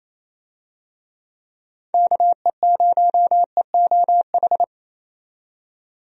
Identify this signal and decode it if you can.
Morse code, heard as a keyed tone, about 28 wpm, 700 Hz. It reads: KE0EOH
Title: USA Call Signs - 6 Characters - 28wpm